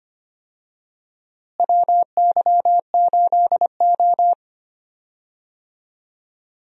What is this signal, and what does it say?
Morse code, heard as a keyed tone, about 25 wpm, 700 Hz. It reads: WY8O
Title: USA Call Signs - 4 Characters - Repeated 3x - 25wpm